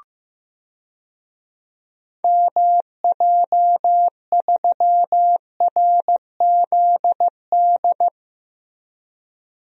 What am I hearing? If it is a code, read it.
Morse code, heard as a keyed tone, about 15 wpm, 700 Hz. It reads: MJ3RZD
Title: DXCC-100 Call Signs - 6 Characters - Repeated 3x - 15wpm